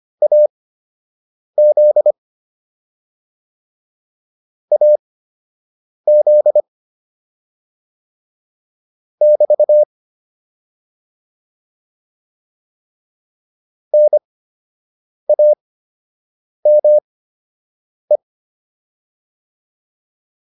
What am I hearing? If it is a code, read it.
Morse code, heard as a keyed tone, about 25 wpm, spaced as for 7 wpm, 600 Hz. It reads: AZ AZ <BT> NAME